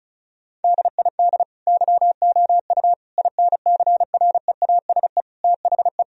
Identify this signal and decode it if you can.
Morse code, heard as a keyed tone, about 36 wpm, 700 Hz. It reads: DID YOU INCREASE THE
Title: Sentences from Top 1000 Words - Review Part 4 - 35wpm